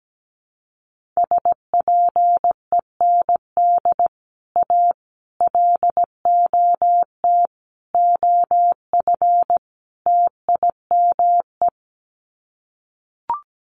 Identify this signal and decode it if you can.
Morse code, heard as a keyed tone, about 17 wpm, 700 Hz. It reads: SPEND A LOT OF TIME
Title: Sets of 5 Words 17wpm